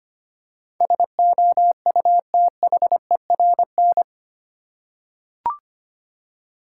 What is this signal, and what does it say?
Morse code, heard as a keyed tone, about 25 wpm, 700 Hz. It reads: SOUTHERN
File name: Eight Letter Words - 25wpm